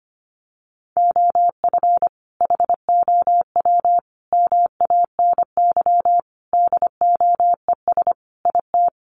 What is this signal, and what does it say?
Morse code, heard as a keyed tone, about 25 wpm, 700 Hz. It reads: OF HOW MANY DOES IT
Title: Sentences from Top 1400 Words 25wpm